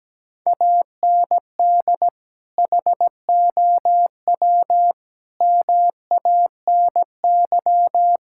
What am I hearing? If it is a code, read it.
Morse code, heard as a keyed tone, about 17 wpm, 700 Hz. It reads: AND HOW MANY
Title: Sentences from Top 1000 Words - Review Part 6 - 17wpm